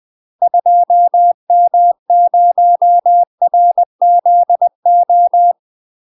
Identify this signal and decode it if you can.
Morse code, heard as a keyed tone, about 20 wpm, 700 Hz. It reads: 2M0RZO
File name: UK MASTER QSOs TRAINING / 20WPM / UK MASTER QSOs 0040